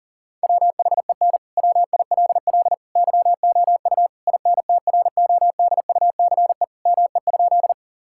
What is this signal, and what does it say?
Morse code, heard as a keyed tone, about 40 wpm, 700 Hz. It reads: WHEN WILL YOU INTRODUCE ME?
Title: Sentences from Top 1600 Words 40wpm